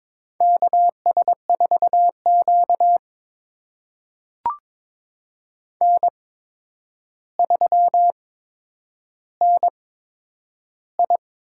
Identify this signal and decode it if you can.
Morse code, heard as a keyed tone, about 21 wpm, spaced as for 9 wpm, 700 Hz. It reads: KS4Q N3NI
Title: USA Call Signs - 4 Characters - 8x Character Spacing - 22wpm